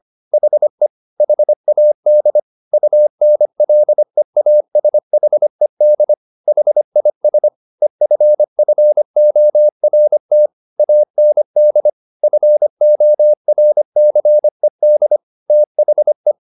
Morse code, heard as a keyed tone, about 25 wpm, 600 Hz. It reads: HE HAD UNLEASHED HIS EFFORT AND FORCED THE